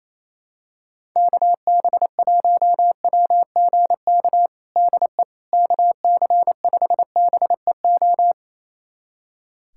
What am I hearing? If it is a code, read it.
Morse code, heard as a keyed tone, about 27 wpm, 700 Hz. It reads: KB1WGK DE KC5BEO
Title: USA Callsign DE Callsign - 6 Characters - 28wpm